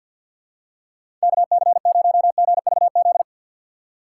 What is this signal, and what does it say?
Morse code, heard as a keyed tone, about 48 wpm, 700 Hz. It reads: KC0GVZ
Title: USA Call Signs - 6 Characters - 6x Character Spacing - 50wpm